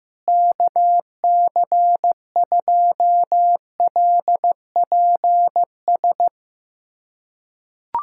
Morse code, heard as a keyed tone, about 16 wpm, 700 Hz. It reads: KC2LPS
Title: USA Callsign DE Callsign 15wpm